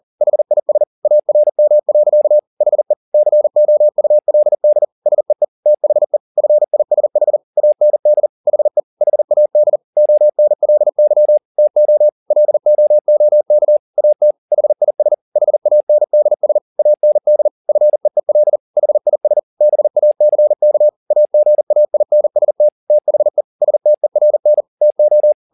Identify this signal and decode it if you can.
Morse code, heard as a keyed tone, about 40 wpm, 600 Hz. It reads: HIS ARM. HE COULD SEE THE FISH AND HE HAD ONLY TO LOOK AT HIS HANDS AND FEEL HIS BACK AGAINST THE STERN TO
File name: the old man and the sea 2 40wpm